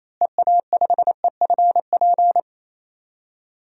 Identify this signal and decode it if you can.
Morse code, heard as a keyed tone, about 28 wpm, 700 Hz. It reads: EA5EFP
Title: DXCC-100 Call Signs - 6 Characters - Repeated 2x - 28wpm